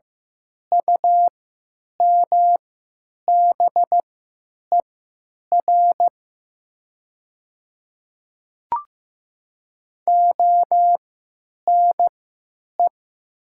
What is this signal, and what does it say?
Morse code, heard as a keyed tone, about 15 wpm, spaced as for 8 wpm, 700 Hz. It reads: UMBER ONE
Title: 3x Character Spacing - Top 100 Words - 15wpm